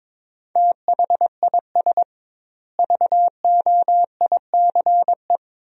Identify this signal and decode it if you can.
Morse code, heard as a keyed tone, about 22 wpm, 700 Hz. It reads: THIS VOICE
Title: 2x Word-Spacing - Sentences from Top 500 Words - 22wpm